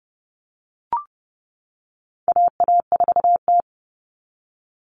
Morse code, heard as a keyed tone, about 30 wpm, 700 Hz. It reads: AA4T
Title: All Cut Numbers - 4-Digit 30wpm